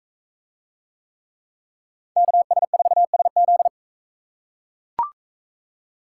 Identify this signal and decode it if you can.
Morse code, heard as a keyed tone, about 42 wpm, 700 Hz. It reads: KS4SZ